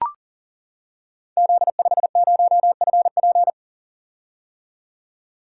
Morse code, heard as a keyed tone, about 41 wpm, 700 Hz. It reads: Z50FP
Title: Code Groups - 5 Letter:Numbers - 40wpm